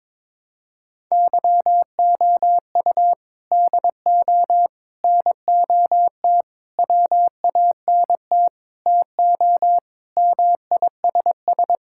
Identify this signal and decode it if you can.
Morse code, heard as a keyed tone, about 22 wpm, 700 Hz. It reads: YOU DO NOT WANT TO MISS
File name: practice / 22wpm / Sentences with 4-Word Phrases - 22wpm